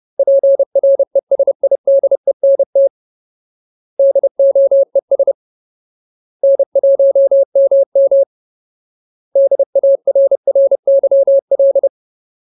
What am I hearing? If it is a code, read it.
Morse code, heard as a keyed tone, about 30 wpm, 550 Hz. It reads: PRESIDENT DOES N1MM DARRYL
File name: Ultimate Word List / 30WPM Ultimate